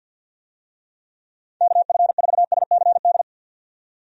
Morse code, heard as a keyed tone, about 52 wpm, 700 Hz. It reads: KF4SCD
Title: USA Call Signs - 6 Characters - 50wpm